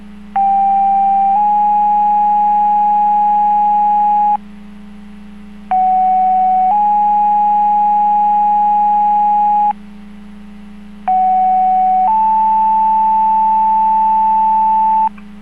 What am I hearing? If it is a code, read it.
Two-tone page, heard as a one-second tone, then a three-second tone, about 790 then 830 Hz, about 750 then 830 Hz, about 750 then 880 Hz. It is heard from 0.4 to 15.1 s.